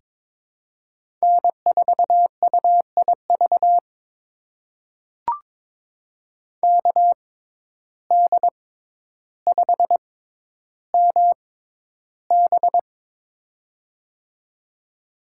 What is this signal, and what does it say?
Morse code, heard as a keyed tone, about 23 wpm, spaced as for 11 wpm, 700 Hz. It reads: N4UIV KD5MB
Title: USA Call Signs - 5 Characters - 6x Character Spacing - 22wpm